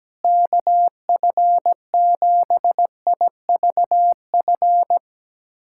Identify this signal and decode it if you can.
Morse code, heard as a keyed tone, about 17 wpm, 700 Hz. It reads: KF7IVF
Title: USA Callsign DE Callsign 17wpm